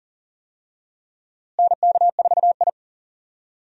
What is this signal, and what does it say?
Morse code, heard as a keyed tone, about 40 wpm, 700 Hz. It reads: NK4I
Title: USA Call Signs - 4 Characters - Repeated 2x - 40wpm